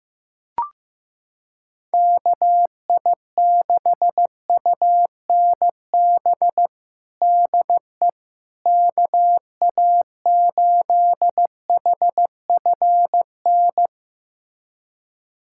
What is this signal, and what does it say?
Morse code, heard as a keyed tone, about 15 wpm, 700 Hz. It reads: KI6UNB DE KA8HFN
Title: USA Callsign DE Callsign - 6 Characters - 15wpm